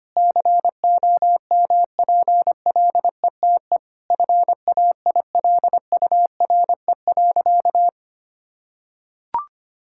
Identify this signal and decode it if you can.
Morse code, heard as a keyed tone, about 25 wpm, 700 Hz. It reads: COMPLETE FAILURE.
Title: Sentences from Top 1000 Words - Review Part 6 - 25wpm